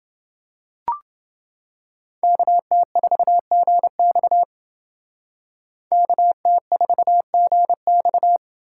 Morse code, heard as a keyed tone, about 29 wpm, 700 Hz. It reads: KT4GX KT4GX